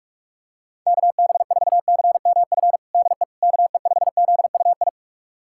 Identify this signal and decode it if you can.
Morse code, heard as a keyed tone, about 42 wpm, 700 Hz. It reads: KB4CMF DE KE5ZUI
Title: USA Callsign DE Callsign 45wpm